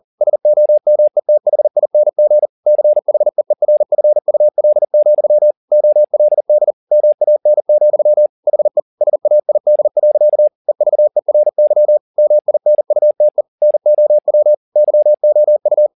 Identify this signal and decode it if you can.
Morse code, heard as a keyed tone, about 40 wpm, 600 Hz. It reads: SOMETHING CHEERFUL, OLD MAN, HE SAID. EVERY MINUTE NOW YOU
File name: the old man and the sea 2 40wpm